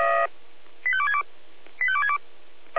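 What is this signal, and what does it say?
Five-tone selective call (ZVEI), heard as quick runs of five tones, tones of about 70 ms. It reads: 85262 85262
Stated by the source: Five-tone ZVEI selective call received on an SDR.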